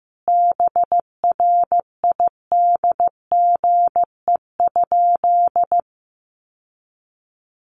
Morse code, heard as a keyed tone, about 15 wpm, 700 Hz. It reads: BRIDGE?
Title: Sentences from Top 1200 Words 15wpm